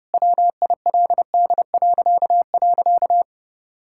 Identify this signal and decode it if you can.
Morse code, heard as a keyed tone, about 30 wpm, 700 Hz. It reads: WILD..
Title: Pangram Challenge v2 - 30wpm